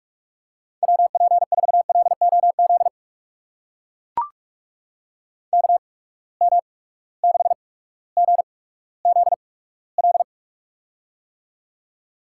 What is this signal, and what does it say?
Morse code, heard as a keyed tone, about 43 wpm, spaced as for 19 wpm, 700 Hz. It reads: WP4LOZ KM6GZL
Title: USA Call Signs - 6 Characters - 8x Character Spacing - 45wpm